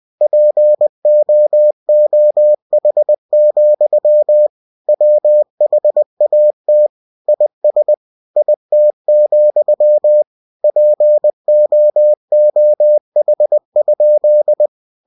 Morse code, heard as a keyed tone, about 20 wpm, 600 Hz. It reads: POOH, WHAT IS IT, POOH?
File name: cw-books / winnie-the-pooh / winnie-20wpm-page-0109